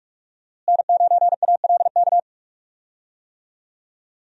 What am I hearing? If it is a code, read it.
Morse code, heard as a keyed tone, about 46 wpm, 700 Hz. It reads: N9ALK
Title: USA Call Signs - 5 Characters - Repeated 3x - 45wpm